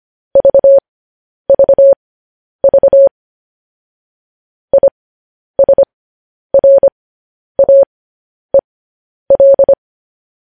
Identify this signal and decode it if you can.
Morse code, heard as a keyed tone, about 25 wpm, spaced as for 10 wpm, 550 Hz. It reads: VVV ISRAEL
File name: podcast / News headlines-1x-25x10